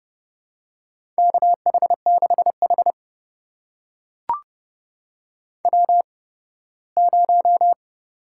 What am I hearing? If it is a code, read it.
Morse code, heard as a keyed tone, about 30 wpm, spaced as for 18 wpm, 700 Hz. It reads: KH6H W0
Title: USA Call Signs - 4 Characters - 8x Character Spacing - 30wpm